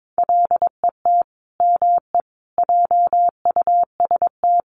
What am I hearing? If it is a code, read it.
Morse code, heard as a keyed tone, about 22 wpm, 700 Hz. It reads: LET ME JUST
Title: Sets of 5 Words Encore2 22wpm